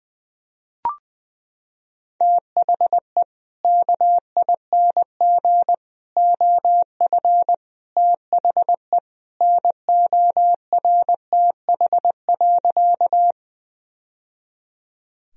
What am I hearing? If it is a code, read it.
Morse code, heard as a keyed tone, about 20 wpm, 700 Hz. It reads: THE KING OF THE NORTH.
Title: Sentences from Top 1000 Words - Review Part 1 - 20wpm